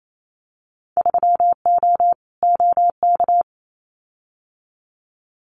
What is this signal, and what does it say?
Morse code, heard as a keyed tone, about 28 wpm, 700 Hz. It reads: 3O OK
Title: Winter Field Day Exchanges 28wpm